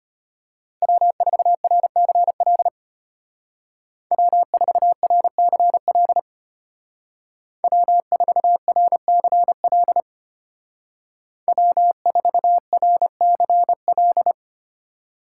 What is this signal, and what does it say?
Morse code, heard as a keyed tone, about 30 wpm, 700 Hz. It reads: W4RCL W4RCL W4RCL W4RCL
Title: Speed Racing - USA 1x3 Call Signs 25wpm